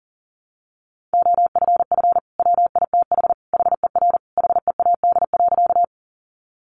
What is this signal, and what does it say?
Morse code, heard as a keyed tone, about 40 wpm, 700 Hz. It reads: OFF WITH HER HEAD.
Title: Sentences from Top 300 Words 40wpm